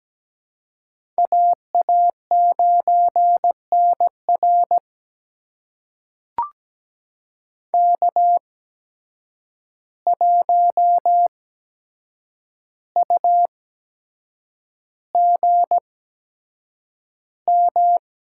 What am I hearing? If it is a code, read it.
Morse code, heard as a keyed tone, about 17 wpm, spaced as for 7 wpm, 700 Hz. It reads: AA9NR K1UGM
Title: USA Call Signs - 5 Characters - 8x Character Spacing - 17wpm